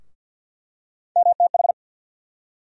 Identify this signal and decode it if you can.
Morse code, heard as a keyed tone, about 48 wpm, 700 Hz. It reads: MTH
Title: USA Call Sign Postfixes - 3 Letters 50wpm